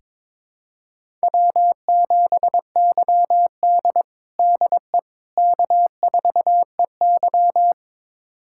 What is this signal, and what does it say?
Morse code, heard as a keyed tone, about 22 wpm, 700 Hz. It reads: W7YD DE K4EY
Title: USA Callsign DE Callsign - 4 Characters - 22wpm